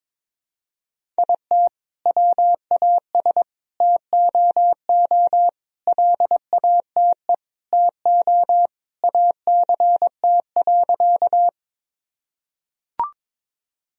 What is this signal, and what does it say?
Morse code, heard as a keyed tone, about 21 wpm, 700 Hz. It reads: IT WAS TOO LATE TO ACT.
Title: Sentences from Top 1000 Words - Review Part 1 - 22wpm